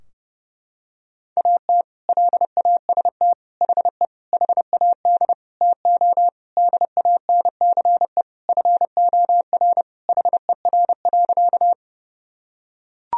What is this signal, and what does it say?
Morse code, heard as a keyed tone, about 32 wpm, 700 Hz. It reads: AT LAST HE HAD TO DANCE FOR HER.
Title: Sentences from Top 1000 Words - Review Part 4 - 30wpm